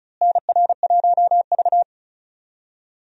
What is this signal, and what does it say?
Morse code, heard as a keyed tone, about 36 wpm, 700 Hz. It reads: NR1V
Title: USA Call Signs - 4 Characters - Repeated 2x - 35wpm